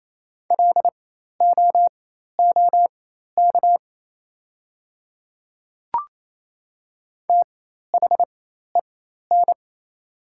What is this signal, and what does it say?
Morse code, heard as a keyed tone, about 28 wpm, spaced as for 13 wpm, 700 Hz. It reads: LOOK THEN
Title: 4x Character Spacing - Top 100 Words - 28wpm